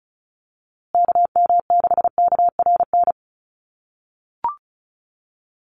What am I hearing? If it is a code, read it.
Morse code, heard as a keyed tone, about 34 wpm, 700 Hz. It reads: KM6KRN
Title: USA Call Signs Repeated 2x 35wpm